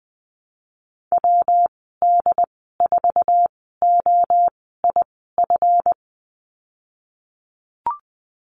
Morse code, heard as a keyed tone, about 20 wpm, spaced as for 15 wpm, 700 Hz. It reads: WD4OIF
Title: 154 Lesson 25g - Letter p - Call Signs 20wpm